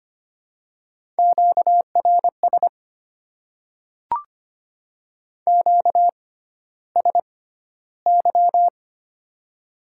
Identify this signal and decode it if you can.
Morse code, heard as a keyed tone, about 25 wpm, spaced as for 13 wpm, 690 Hz. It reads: QRS QSY
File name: Top 90 QSO Elements - 6x Character Spacing - 25wpm